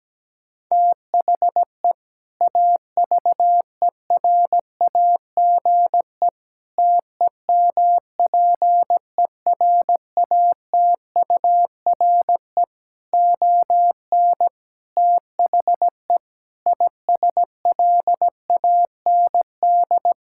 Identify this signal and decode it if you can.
Morse code, heard as a keyed tone, about 17 wpm, 700 Hz. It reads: THE AVERAGE TEMPERATURE ON THE ISLAND
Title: Sentences from Top 1000 Words - Review Part 6 - 17wpm